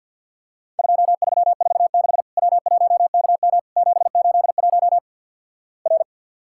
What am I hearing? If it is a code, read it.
Morse code, heard as a keyed tone, about 50 wpm, 700 Hz. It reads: 2346 W1KM 781 R
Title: Worked All Europe DX Contest - QTC Exchange - 50wpm